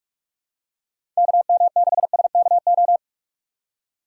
Morse code, heard as a keyed tone, about 44 wpm, 700 Hz. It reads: KM6SKO
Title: USA Call Signs - 6x Character Spacing - 45wpm